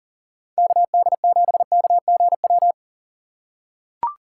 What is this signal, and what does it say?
Morse code, heard as a keyed tone, about 39 wpm, 700 Hz. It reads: KD7KGW
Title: USA Call Signs - 6 Characters - 6x Character Spacing - 40wpm